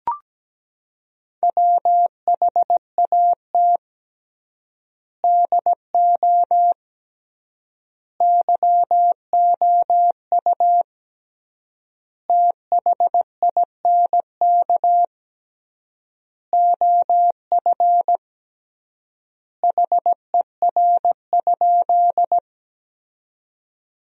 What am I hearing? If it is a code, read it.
Morse code, heard as a keyed tone, about 17 wpm, 700 Hz. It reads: WHAT DO YOU THINK OF HER?
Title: 3x Word-Spacing - Sentences from Top 200 Words - 17wpm